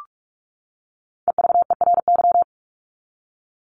Morse code, heard as a keyed tone, about 44 wpm, 700 Hz. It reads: EVERY
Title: Top 300 Words 45wpm